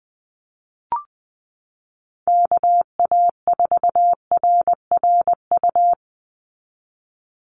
Morse code, heard as a keyed tone, about 20 wpm, 700 Hz. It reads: KA4RRU
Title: Call Signs Repeated 3x - 20wpm